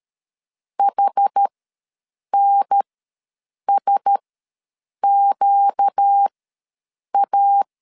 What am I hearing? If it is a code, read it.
Morse code, heard as a keyed tone, about 13 wpm, spaced as for 7 wpm, 800 Hz. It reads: HNSQA
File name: LEZIONI / 6 PKX0 / 6 P ws 12wpm 3117